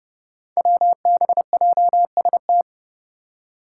Morse code, heard as a keyed tone, about 32 wpm, 700 Hz. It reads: WBJST
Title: Five Challenging Letter-Number Combinations 30wpm